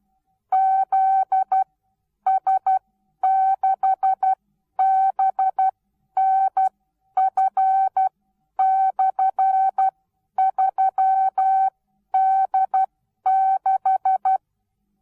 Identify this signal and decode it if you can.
Morse code, heard as a keyed tone, about 12 wpm, spaced as for 10 wpm, 770 Hz. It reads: ZS6BNF/3D6
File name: Onlinelehrgang / morsen / DJ4UF Morsen19